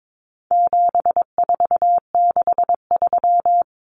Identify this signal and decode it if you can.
Morse code, heard as a keyed tone, about 22 wpm, 700 Hz. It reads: 7463
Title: Common Cut Numbers - 4-Digit 22wpm